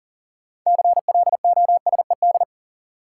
Morse code, heard as a keyed tone, about 40 wpm, 700 Hz. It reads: CLOSED
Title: Sentences from Top 1400 Words - New Word Drill - 40wpm